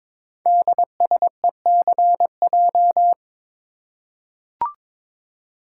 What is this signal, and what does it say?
Morse code, heard as a keyed tone, about 22 wpm, 700 Hz. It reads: DSECJ
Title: Code Groups - 5 Letters - 22wpm